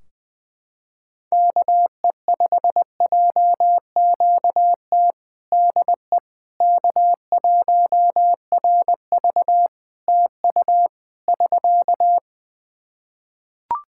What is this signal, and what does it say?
Morse code, heard as a keyed tone, about 20 wpm, 700 Hz. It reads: KE5JQT DE K1RV TU <SK>
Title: Practice QSO - Exchange 4 - Ending - 20wpm